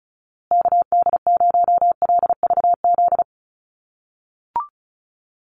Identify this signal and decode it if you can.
Morse code, heard as a keyed tone, about 34 wpm, 700 Hz. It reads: KD0LVZ
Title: USA Call Signs Repeated 3x 35wpm